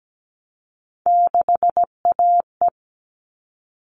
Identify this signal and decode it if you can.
Morse code, heard as a keyed tone, about 17 wpm, 700 Hz. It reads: 6AE
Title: All Cut Numbers - 3-Digit 17wpm